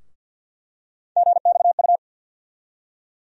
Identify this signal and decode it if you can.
Morse code, heard as a keyed tone, about 47 wpm, 700 Hz. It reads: GXU
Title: USA Call Sign Postfixes - 3 Letters 50wpm